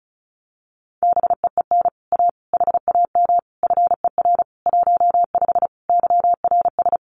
Morse code, heard as a keyed tone, about 34 wpm, 700 Hz. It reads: BEEN A HAM FER 15 YRS
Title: Common QSO Phrases Encore 35wpm